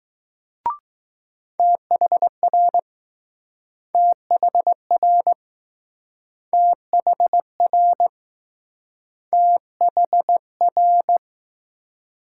Morse code, heard as a keyed tone, about 20 wpm, 700 Hz. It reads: THR THR THR THR